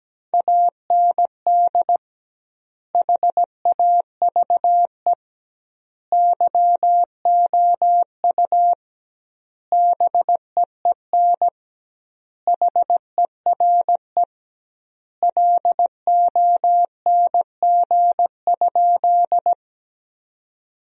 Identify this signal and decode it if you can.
Morse code, heard as a keyed tone, about 17 wpm, 700 Hz. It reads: AND HAVE YOU BEEN HERE LONG?
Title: 2x Word-Spacing - Sentences from Top 200 Words - 17wpm